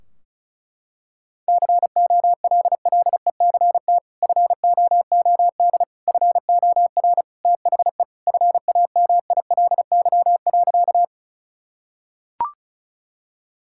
Morse code, heard as a keyed tone, about 36 wpm, 700 Hz. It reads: COLLECT FOOD FOR THE FAMILY.
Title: Sentences from Top 1600 Words 35wpm